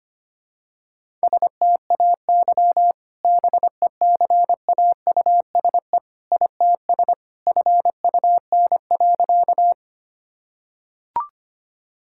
Morse code, heard as a keyed tone, about 25 wpm, 700 Hz. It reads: STAY BECAUSE ITS FUN.
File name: practice / 25wpm / Sentences with 2-Word Phrases - 25wpm